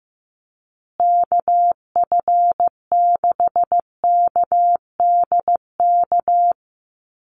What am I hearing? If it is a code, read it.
Morse code, heard as a keyed tone, about 15 wpm, 700 Hz. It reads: KF6KDK
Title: USA Call Signs Repeated 3x 15wpm